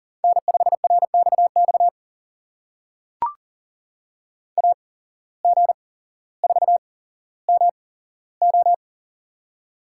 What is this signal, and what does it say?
Morse code, heard as a keyed tone, about 41 wpm, spaced as for 17 wpm, 700 Hz. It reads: N5RXX AG4MO
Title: USA Call Signs - 5 Characters - 8x Character Spacing - 40wpm